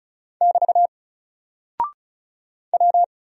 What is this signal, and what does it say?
Morse code, heard as a keyed tone, about 33 wpm, 700 Hz. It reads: <BT> W
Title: Single Character (Letter:Number:Punctuation:Prosign) Alt 35wpm